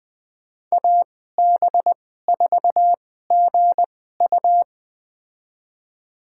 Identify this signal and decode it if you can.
Morse code, heard as a keyed tone, about 20 wpm, spaced as for 14 wpm, 700 Hz. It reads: AB4GU
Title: USA Call Signs - 5 Characters - 2x Character Spacing - 20wpm